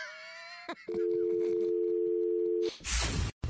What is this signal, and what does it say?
Dial tone (North America), 350 + 440 Hz.